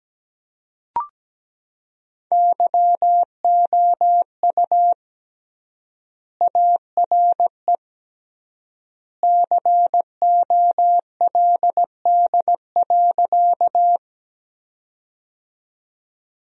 Morse code, heard as a keyed tone, about 17 wpm, 700 Hz. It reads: YOU ARE COLD.
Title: 3x Word-Spacing - Sentences from Top 400 Words - 17wpm